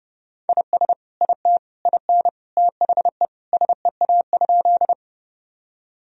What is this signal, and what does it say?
Morse code, heard as a keyed tone, about 30 wpm, 700 Hz. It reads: IS IT IN THE SEA?
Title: Sentences from Top 1000 Words - Review Part 1 - 30wpm